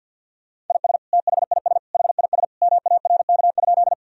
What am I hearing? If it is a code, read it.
Morse code, heard as a keyed tone, about 52 wpm, 700 Hz. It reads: IS THIS HIS MARK?
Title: Sentences from Top 1000 Words - Review Part 1 - 50wpm